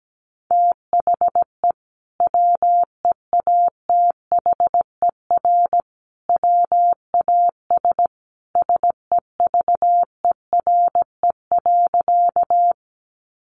Morse code, heard as a keyed tone, about 17 wpm, 700 Hz. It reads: THE WEATHER WAS SEVERE.
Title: Sentences from Top 1400 Words 17wpm